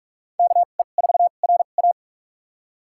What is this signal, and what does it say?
Morse code, heard as a keyed tone, about 46 wpm, spaced as for 32 wpm, 700 Hz. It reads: KE4RA
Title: USA Call Signs - 2x Character Spacing - 45wpm